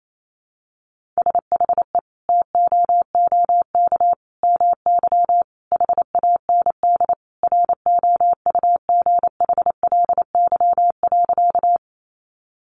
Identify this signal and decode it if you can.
Morse code, heard as a keyed tone, about 28 wpm, 700 Hz. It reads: SHE TOOK MY HAND ROUGHLY.
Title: Sentences from Top 1500 Words 28wpm